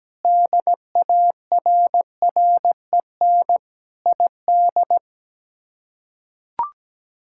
Morse code, heard as a keyed tone, about 17 wpm, 700 Hz. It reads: DARREN ID
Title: K1USN SST QSO Practice - 17wpm